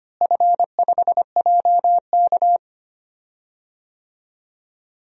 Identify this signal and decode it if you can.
Morse code, heard as a keyed tone, about 25 wpm, 700 Hz. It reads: F5JK
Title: DXCC-100 Call Signs - 4 Characters - Repeated 3x - 25wpm